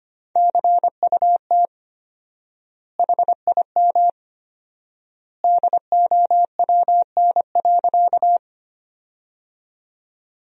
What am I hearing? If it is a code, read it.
Morse code, heard as a keyed tone, about 25 wpm, 700 Hz. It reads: CUT HIM DOWN.
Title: 4x Word-Spacing - Sentences from Top 300 Words - 25wpm